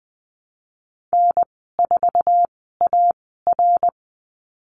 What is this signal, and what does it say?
Morse code, heard as a keyed tone, about 20 wpm, spaced as for 15 wpm, 700 Hz. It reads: N4AR
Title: 082 Lesson 14g - Number 2 - Call Signs 20wpm